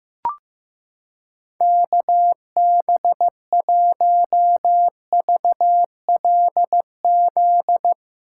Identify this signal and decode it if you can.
Morse code, heard as a keyed tone, about 15 wpm, 700 Hz. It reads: KB1VLZ
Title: USA Callsign DE Callsign 15wpm